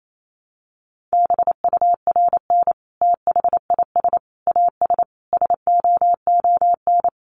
Morse code, heard as a keyed tone, about 28 wpm, 700 Hz. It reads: BURN THIS AS SOON
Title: Sentences from Top 1300 Words 28wpm